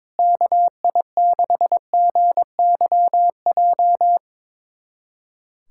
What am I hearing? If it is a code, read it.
Morse code, heard as a keyed tone, about 22 wpm, 700 Hz. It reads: KI6GYJ
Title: USA Callsign DE Callsign 22wpm